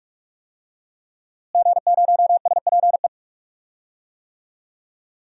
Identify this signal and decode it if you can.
Morse code, heard as a keyed tone, about 41 wpm, 700 Hz. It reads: G0SPE
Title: DXCC-100 Call Signs - 5 Characters - Repeated 3x - 45wpm